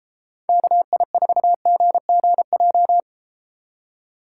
Morse code, heard as a keyed tone, about 33 wpm, 700 Hz. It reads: KI4GGJ